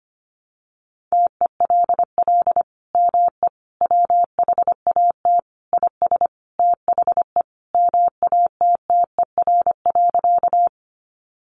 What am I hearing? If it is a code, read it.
Morse code, heard as a keyed tone, about 25 wpm, 700 Hz. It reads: TELL ME WHAT IS THE MATTER.
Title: Sentences from Top 700 Words 25wpm